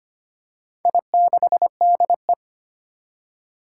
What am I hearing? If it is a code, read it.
Morse code, heard as a keyed tone, about 25 wpm, 700 Hz. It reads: I6DE
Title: Four Challenging Letter-Number Combinations 25wpm